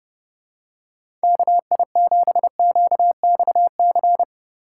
Morse code, heard as a keyed tone, about 29 wpm, 700 Hz. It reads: KI7QXC